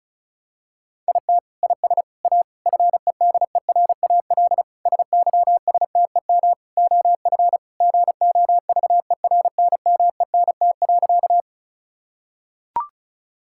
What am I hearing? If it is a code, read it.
Morse code, heard as a keyed tone, about 36 wpm, 700 Hz. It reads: IT IS A FEDERAL SYSTEM OF GOVERNMENT.
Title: Sentences from Top 1000 Words - Review Part 5 - 35wpm